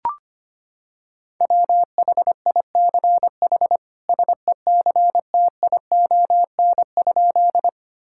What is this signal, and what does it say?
Morse code, heard as a keyed tone, about 25 wpm, 700 Hz. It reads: WHICH SECTION?
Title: Sentences from Top 1000 Words - Review Part 5 - 25wpm